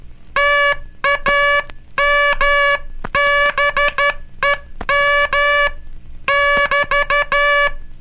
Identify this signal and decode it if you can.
Morse code, heard as a keyed tone, about 12 wpm, 1700 Hz. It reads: TAMBEM <BT>
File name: cursos / Curso CW mp3 / CW Aula - 18